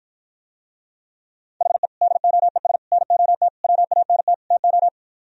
Morse code, heard as a keyed tone, about 53 wpm, 700 Hz. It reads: HE DOES NOT WANT TO